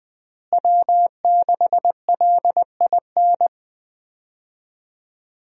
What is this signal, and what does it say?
Morse code, heard as a keyed tone, about 21 wpm, 700 Hz. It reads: W6LIN
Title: USA Call Signs - 5 Characters - Repeated 2x - 20wpm